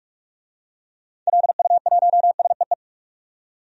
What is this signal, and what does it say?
Morse code, heard as a keyed tone, about 46 wpm, 700 Hz. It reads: LU1SEE
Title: DXCC-100 Call Signs - 6 Characters - 45wpm